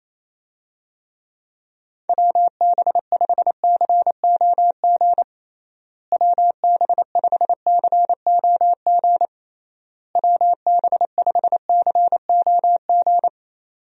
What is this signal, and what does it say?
Morse code, heard as a keyed tone, about 28 wpm, 700 Hz. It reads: WB5COG WB5COG WB5COG
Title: USA Call Signs - 6 Characters - Repeated 3x - 28wpm